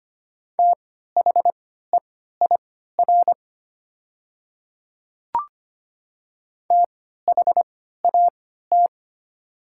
Morse code, heard as a keyed tone, about 25 wpm, spaced as for 14 wpm, 700 Hz. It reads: THEIR THAT
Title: 3x Character Spacing - Top 100 Words - 25wpm